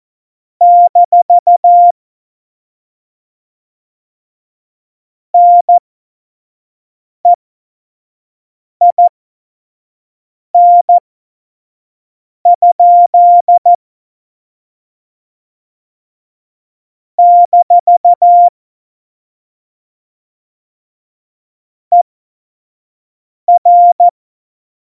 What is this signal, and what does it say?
Morse code, heard as a keyed tone, about 14 wpm, spaced as for 5 wpm, 700 Hz. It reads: - NEIN? - ER
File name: Morsekurs / Witze / Witz 17. November 2020, 25 F70